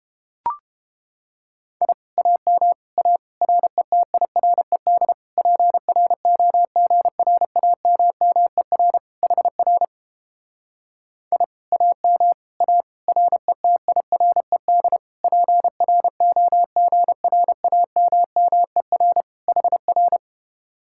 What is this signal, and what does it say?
Morse code, heard as a keyed tone, about 29 wpm, 700 Hz. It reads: I AM A RETIRED PROGRAMMER HR I AM A RETIRED PROGRAMMER HR